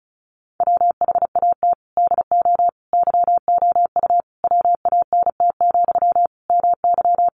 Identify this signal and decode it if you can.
Morse code, heard as a keyed tone, about 34 wpm, 700 Hz. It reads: WHAT DO YOU WANT, MY
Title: Sentences from Top 1000 Words 35wpm